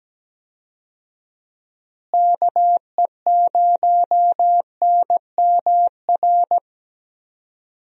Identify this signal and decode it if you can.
Morse code, heard as a keyed tone, about 17 wpm, 700 Hz. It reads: KE0NMR
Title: USA Call Signs - 6 Characters - Repeated 2x - 17wpm